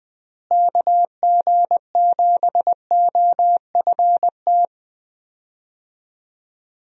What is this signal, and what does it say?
Morse code, heard as a keyed tone, about 20 wpm, 700 Hz. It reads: KG7OFT
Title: USA Call Signs - 6 Characters - Repeated 3x - 20wpm